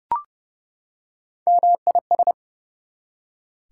Morse code, heard as a keyed tone, about 30 wpm, 700 Hz. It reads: MIS
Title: Prefixes and Suffixes 30wpm